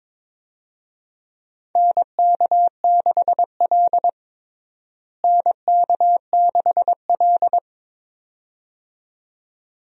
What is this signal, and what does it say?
Morse code, heard as a keyed tone, about 22 wpm, 700 Hz. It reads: NK6L NK6L
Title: USA Call Signs - 4 Characters - Repeated 2x - 22wpm